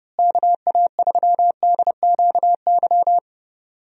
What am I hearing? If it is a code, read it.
Morse code, heard as a keyed tone, about 30 wpm, 700 Hz. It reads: KA3DQY
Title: USA Call Signs - 6 Characters - Repeated 3x - 30wpm